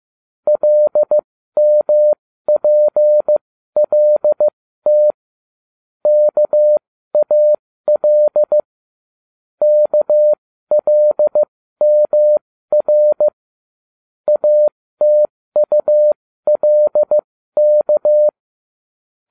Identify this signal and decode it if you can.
Morse code, heard as a keyed tone, about 15 wpm, spaced as for 13 wpm, 600 Hz. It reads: LMPLT KAL KLMR ATULK